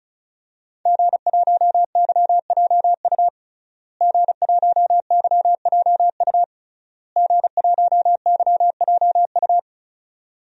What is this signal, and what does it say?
Morse code, heard as a keyed tone, about 35 wpm, 700 Hz. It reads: G1YJU G1YJU G1YJU
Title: DXCC-100 Call Signs - 5 Characters - Repeated 3x - 35wpm